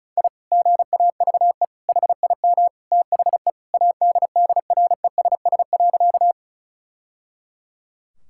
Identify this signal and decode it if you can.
Morse code, heard as a keyed tone, about 34 wpm, 690 Hz. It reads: I GAVE HIM THE ADDRESS.
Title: Sentences from Top 1000 Words - Review Part 6 - 35wpm